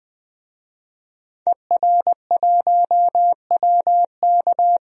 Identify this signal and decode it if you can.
Morse code, heard as a keyed tone, about 21 wpm, 700 Hz. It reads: ER1WK
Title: DXCC-100 Call Signs - 5 Characters - Repeated 3x - 20wpm